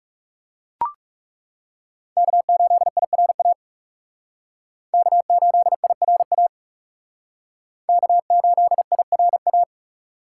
Morse code, heard as a keyed tone, about 36 wpm, 700 Hz. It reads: K8IRA K8IRA K8IRA
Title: Speed Racing - USA 1x3 Call Signs 30wpm